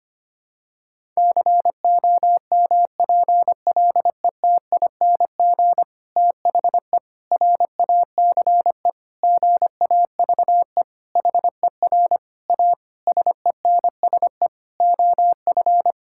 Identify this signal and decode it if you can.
Morse code, heard as a keyed tone, about 25 wpm, 700 Hz. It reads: COMPLETING THE RACE GAVE HER A SENSE OF
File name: practice / 25wpm / Difficult Sentences with Hints - 25wpm